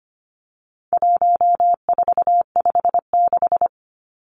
Morse code, heard as a keyed tone, about 25 wpm, 700 Hz. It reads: 1456
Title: Numbers - 4-Digit 25wpm